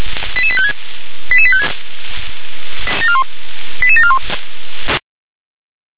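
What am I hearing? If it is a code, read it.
Five-tone selective call (ZVEI), heard as quick runs of five tones, tones of about 70 ms. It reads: 88856 88856 88741
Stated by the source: SDR reception of ZVEI five-tone call 88765.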